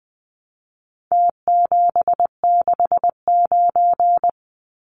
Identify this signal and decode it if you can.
Morse code, heard as a keyed tone, about 20 wpm, 700 Hz. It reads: T769
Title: Numbers - Contest Style - 4-Digit 20wpm